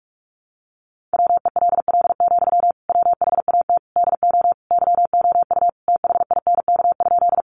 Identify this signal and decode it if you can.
Morse code, heard as a keyed tone, about 44 wpm, 700 Hz. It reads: WELL, WHAT DO YOU THINK?
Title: Sentences from Top 200 Words 45wpm